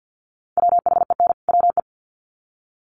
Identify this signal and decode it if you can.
Morse code, heard as a keyed tone, about 50 wpm, 700 Hz. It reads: WHEN WE
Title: Sets of 2 Words 50wpm